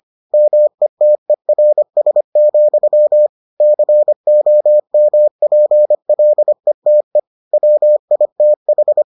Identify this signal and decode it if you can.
Morse code, heard as a keyed tone, about 25 wpm, 600 Hz. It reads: METERS, COMPLETE WITH